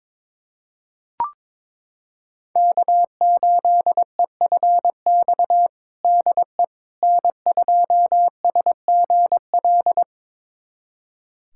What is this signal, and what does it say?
Morse code, heard as a keyed tone, about 21 wpm, 700 Hz. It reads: K8EFX DE N2SGL
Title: USA Callsign DE Callsign - 5 Characters - 22wpm